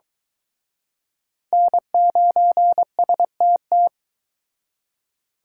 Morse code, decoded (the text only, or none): N9STT